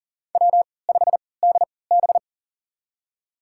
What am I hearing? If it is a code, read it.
Morse code, heard as a keyed tone, about 40 wpm, spaced as for 23 wpm, 700 Hz. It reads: W5DB
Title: USA Call Signs - 4 Characters - 3x Character Spacing - 40wpm